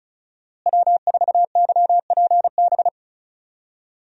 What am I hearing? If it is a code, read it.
Morse code, heard as a keyed tone, about 36 wpm, 700 Hz. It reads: W4YPB
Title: USA Call Signs - 5 Characters - Repeated 2x - 35wpm